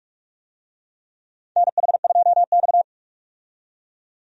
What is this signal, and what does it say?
Morse code, heard as a keyed tone, about 44 wpm, 700 Hz. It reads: NH2X
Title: USA Call Signs - 4 Characters - 45wpm